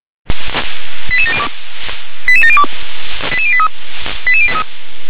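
Five-tone selective call (ZVEI), heard as quick runs of five tones, tones of about 70 ms. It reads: 88702 88702 88083 88083
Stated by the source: ZVEI five-tone selective call received via SDR.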